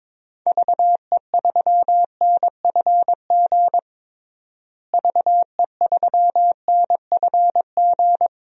Morse code, heard as a keyed tone, about 22 wpm, 700 Hz. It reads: VE3NFG VE3NFG
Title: DXCC-100 Call Signs - 6 Characters - Repeated 3x - 22wpm